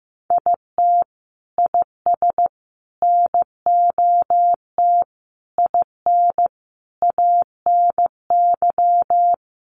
Morse code, heard as a keyed tone, about 15 wpm, 700 Hz. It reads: IT IS NOT IN ANY
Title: Sentences from Top 1200 Words 15wpm